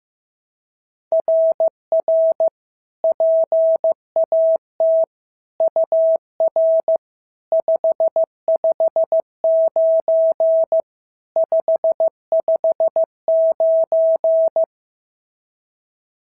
Morse code, heard as a keyed tone, about 13 wpm, 650 Hz. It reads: RR PAT UR 559 559
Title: SOTA QSO Practice 15wpm